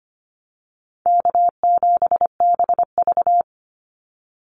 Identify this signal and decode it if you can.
Morse code, heard as a keyed tone, about 25 wpm, 700 Hz. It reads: K7BV
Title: Call Signs Repeated 2x - 25wpm